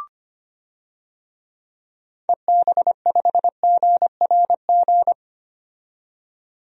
Morse code, heard as a keyed tone, about 25 wpm, 700 Hz. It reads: EB5GRG
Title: DXCC-100 Call Signs - 6 Characters - 25wpm